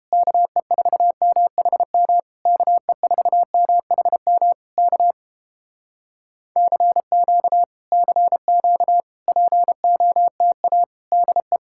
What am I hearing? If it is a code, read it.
Morse code, heard as a keyed tone, about 29 wpm, 700 Hz. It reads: KE4MHM KE4MHM K CQ CQ POTA DE